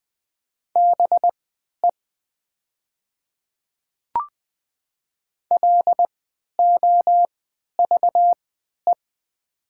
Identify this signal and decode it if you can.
Morse code, heard as a keyed tone, about 20 wpm, spaced as for 11 wpm, 690 Hz. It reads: BE LOVE
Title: 3x Character Spacing - Top 500 Words - 20wpm